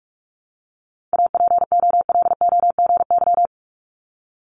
Morse code, heard as a keyed tone, about 46 wpm, 700 Hz. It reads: APOLOGY
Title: Multi-letter Phonemes 45wpm